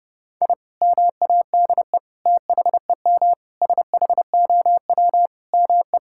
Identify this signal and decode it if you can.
Morse code, heard as a keyed tone, about 30 wpm, 700 Hz. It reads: I MADE THEM SHOW ME
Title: Sentences from Top 1000 Words - Review Part 6 - 30wpm